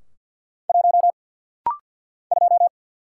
Morse code, heard as a keyed tone, about 48 wpm, 700 Hz. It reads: 1 2
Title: Single Letter-Number - Rapid-Fire 50wpm